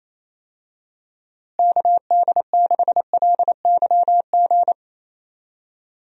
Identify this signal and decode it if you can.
Morse code, heard as a keyed tone, about 28 wpm, 700 Hz. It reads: KD6LYG
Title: USA Call Signs - 6 Characters - Repeated 2x - 28wpm